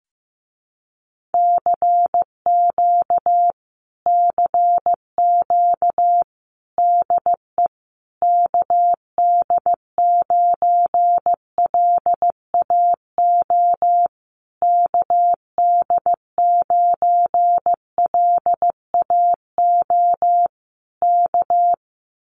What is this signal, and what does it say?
Morse code, heard as a keyed tone, about 15 wpm, 700 Hz. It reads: CQ CQ DE KD9LAO KD9LAO K
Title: Common QSO Phrases Encore 15wpm